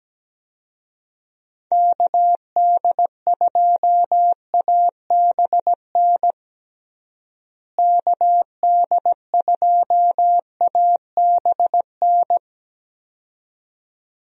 Morse code, heard as a keyed tone, about 17 wpm, 700 Hz. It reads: KD2ABN KD2ABN
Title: USA Call Signs - 6 Characters - Repeated 2x - 17wpm